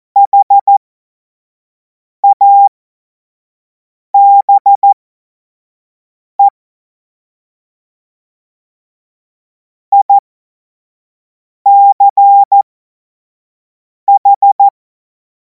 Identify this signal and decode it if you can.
Morse code, heard as a keyed tone, about 14 wpm, spaced as for 5 wpm, 800 Hz. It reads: HABE ICH
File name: Morsekurs / Witze / Witz 10. Februar 2026 25 F70